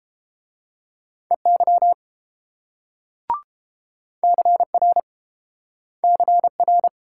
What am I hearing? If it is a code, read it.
Morse code, heard as a keyed tone, about 30 wpm, 700 Hz. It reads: EY CR CR